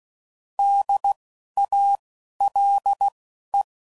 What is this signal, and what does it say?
Morse code, heard as a keyed tone, about 16 wpm, spaced as for 11 wpm, 800 Hz. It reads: DALE
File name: Prefixes / By First Letter / Ds